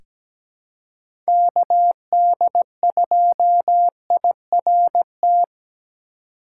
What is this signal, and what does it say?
Morse code, heard as a keyed tone, about 17 wpm, 700 Hz. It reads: KD2IRT
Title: USA Call Signs - 4x Character Spacing - 17wpm